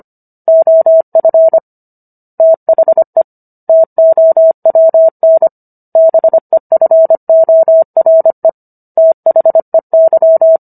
Morse code, heard as a keyed tone, about 25 wpm, 650 Hz. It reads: OF THE TOWN BEFORE THEY